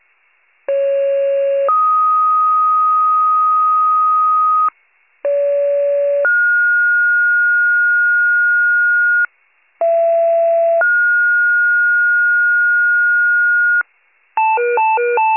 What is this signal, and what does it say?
Two-tone page, heard as a one-second tone, then a three-second tone, about 550 then 1250 Hz, about 570 then 1430 Hz, about 670 then 1430 Hz, with noise. It is heard from 0.7 to 13.8 s.